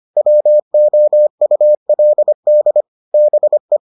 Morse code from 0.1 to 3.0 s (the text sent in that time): WOULD